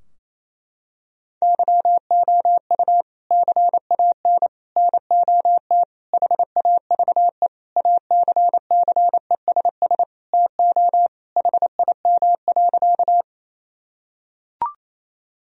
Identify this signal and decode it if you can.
Morse code, heard as a keyed tone, about 27 wpm, 700 Hz. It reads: YOU CAN NOT HAVE ACCESS TO HIM.
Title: Sentences from Top 1000 Words - Review Part 4 - 28wpm